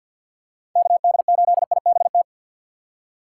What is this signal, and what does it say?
Morse code, heard as a keyed tone, about 51 wpm, 700 Hz. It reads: KD8IBT
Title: USA Call Signs - 6 Characters - 4x Character Spacing - 50wpm